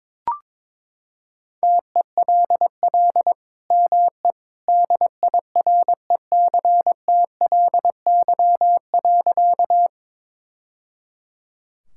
Morse code, heard as a keyed tone, about 22 wpm, 700 Hz. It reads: TELL ME DIRECTLY.
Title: Sentences from Top 1000 Words - Review Part 5 - 22wpm